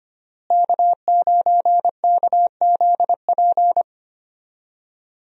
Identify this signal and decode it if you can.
Morse code, heard as a keyed tone, about 25 wpm, 700 Hz. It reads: K9KZP
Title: USA Call Signs - 5 Characters - Repeated 3x - 25wpm